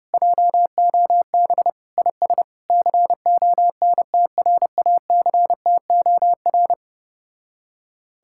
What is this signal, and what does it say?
Morse code, heard as a keyed tone, about 33 wpm, 700 Hz. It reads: JOB IS CONTRACTOR